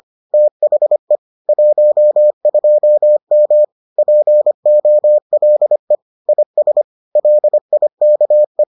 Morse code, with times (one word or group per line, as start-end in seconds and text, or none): 0.3-1.2 THE
1.5-3.6 12M
4.0-6.0 POLE
6.3-6.8 IS
7.2-8.6 LIKE